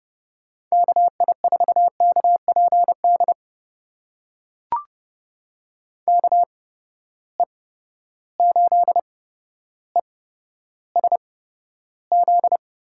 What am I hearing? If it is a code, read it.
Morse code, heard as a keyed tone, about 31 wpm, spaced as for 13 wpm, 700 Hz. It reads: KI4KPD KE8ESZ
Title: USA Call Signs - 6 Characters - 8x Character Spacing - 30wpm